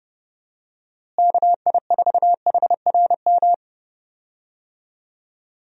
Morse code, heard as a keyed tone, about 30 wpm, 700 Hz. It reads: KI4HRM
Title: USA Call Signs - 6 Characters - Repeated 3x - 30wpm